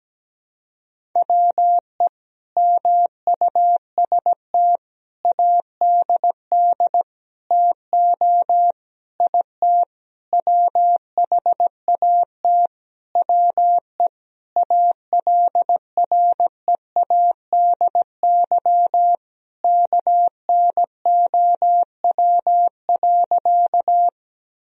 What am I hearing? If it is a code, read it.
Morse code, heard as a keyed tone, about 17 wpm, 700 Hz. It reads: WE MUST ADD TO IT WHAT WE ALREADY KNOW.
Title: Sentences from Top 1000 Words - Review Part 1 - 17wpm